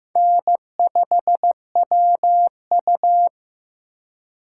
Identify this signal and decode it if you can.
Morse code, heard as a keyed tone, about 15 wpm, 700 Hz. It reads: N5WU
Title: USA Call Signs - 4 Characters - 2x Character Spacing - 15wpm